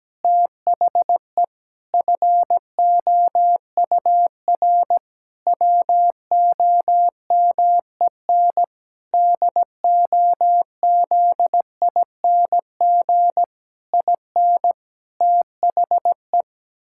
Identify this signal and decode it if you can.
Morse code, heard as a keyed tone, about 17 wpm, 700 Hz. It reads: THE FOUR WOMEN DOZING IN THE
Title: Pangram Challenge v2 - 17wpm